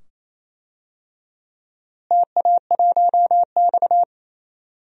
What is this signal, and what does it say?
Morse code, heard as a keyed tone, about 28 wpm, 700 Hz. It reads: TA1X
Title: DXCC-100 Call Signs - 4 Characters - Repeated 2x - 28wpm